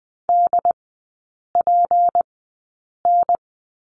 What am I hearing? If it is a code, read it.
Morse code, heard as a keyed tone, about 20 wpm, spaced as for 8 wpm, 700 Hz. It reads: DPN
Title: 166 Lesson 27e - Number 7 - Three Characters 20wpm